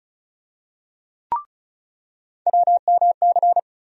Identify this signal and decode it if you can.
Morse code, heard as a keyed tone, about 36 wpm, 700 Hz. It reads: WMC
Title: USA Call Sign Postfixes - 3 Letters 35wpm